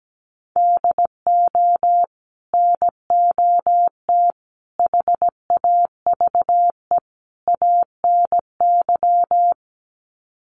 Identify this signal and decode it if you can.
Morse code, heard as a keyed tone, about 17 wpm, 700 Hz. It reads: DO NOT HAVE ANY
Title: Sets of 4 Words 17wpm